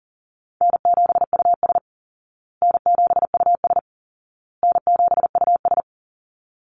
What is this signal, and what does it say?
Morse code, heard as a keyed tone, about 40 wpm, 700 Hz. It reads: N7US N7US N7US
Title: Call Signs Repeated 2x - 40wpm